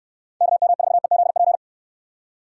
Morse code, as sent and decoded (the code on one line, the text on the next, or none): -.- -- ....- . --.. .--.
KM4EZP